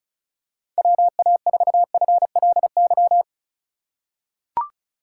Morse code, heard as a keyed tone, about 36 wpm, 700 Hz. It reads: WA4FLY
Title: USA Call Signs - 6 Characters - 2x Character Spacing - 35wpm